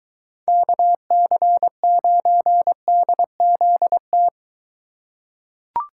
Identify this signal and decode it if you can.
Morse code, heard as a keyed tone, about 23 wpm, 700 Hz. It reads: KC9DZT